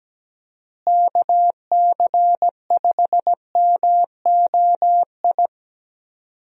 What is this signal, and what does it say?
Morse code, heard as a keyed tone, about 17 wpm, 700 Hz. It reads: KC5MOI
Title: USA Call Signs - 6 Characters - Repeated 3x - 17wpm